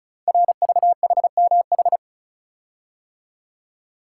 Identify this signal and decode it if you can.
Morse code, heard as a keyed tone, about 36 wpm, 700 Hz. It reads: RVHMH
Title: Code Groups - 5 Letters - 35wpm